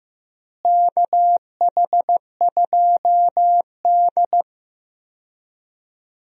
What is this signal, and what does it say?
Morse code, heard as a keyed tone, about 15 wpm, 700 Hz. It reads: KH2D
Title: USA Call Signs - 4 Characters - Repeated 2x - 15wpm